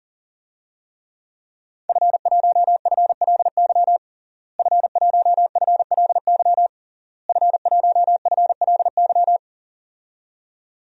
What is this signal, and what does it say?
Morse code, heard as a keyed tone, about 37 wpm, 700 Hz. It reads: F1FLY F1FLY F1FLY
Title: DXCC-100 Call Signs - 5 Characters - Repeated 3x - 40wpm